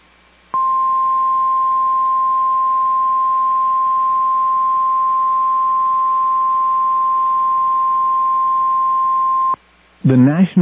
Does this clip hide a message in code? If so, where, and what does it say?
Yes: Emergency Alert System (SAME) tones (the 1050 Hz attention tone) from 0.5 to 9.6 s, with speech.